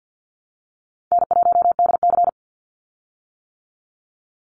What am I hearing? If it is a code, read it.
Morse code, heard as a keyed tone, about 50 wpm, 700 Hz. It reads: N1DC
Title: Call Signs - 50wpm